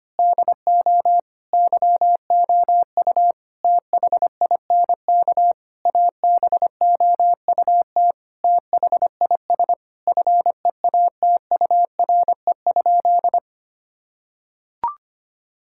Morse code, heard as a keyed tone, about 25 wpm, 700 Hz. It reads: DO YOU THINK ABOUT THIS FEATURE?
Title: Sentences from Top 1000 Words - Review Part 5 - 25wpm